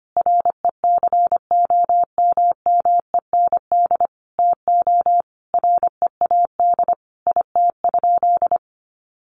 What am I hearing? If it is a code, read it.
Morse code, heard as a keyed tone, about 25 wpm, 700 Hz. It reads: RECOMMEND TO READ IT?
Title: Sentences from Top 1200 Words 25wpm